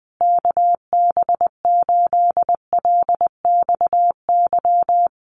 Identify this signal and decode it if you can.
Morse code, heard as a keyed tone, about 20 wpm, 700 Hz. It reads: KB8LXY
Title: Common QSO Phrases Encore 20wpm